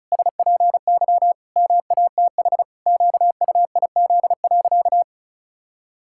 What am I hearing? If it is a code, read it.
Morse code, heard as a keyed tone, about 36 wpm, 700 Hz. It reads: SPY MATH QUIZ.
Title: Pangram Challenge v2 - 35wpm